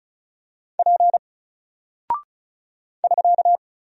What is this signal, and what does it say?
Morse code, heard as a keyed tone, about 36 wpm, 700 Hz. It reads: P <SK>
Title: Single Character (Letter:Number:Punctuation:Prosign) Alt 35wpm